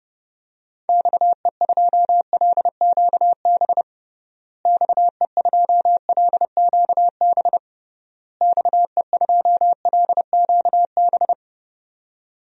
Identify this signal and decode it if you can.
Morse code, heard as a keyed tone, about 30 wpm, 700 Hz. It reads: XE2LQB XE2LQB XE2LQB
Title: DXCC-100 Call Signs - 6 Characters - Repeated 3x - 30wpm